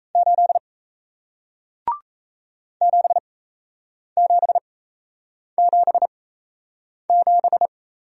Morse code, heard as a keyed tone, about 38 wpm, spaced as for 8 wpm, 700 Hz. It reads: 8 7777